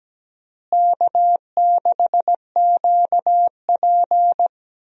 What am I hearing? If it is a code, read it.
Morse code, heard as a keyed tone, about 17 wpm, 700 Hz. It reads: K6QP
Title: USA Call Signs - 4 Characters - Repeated 2x - 17wpm